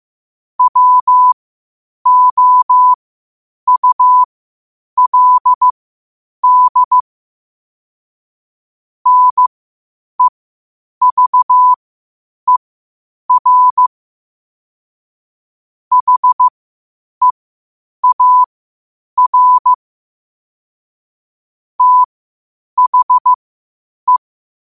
Morse code, heard as a keyed tone, about 15 wpm, spaced as for 8 wpm, 1000 Hz. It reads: WOULD NEVER HEAR THE
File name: qotdfiles / 8wpm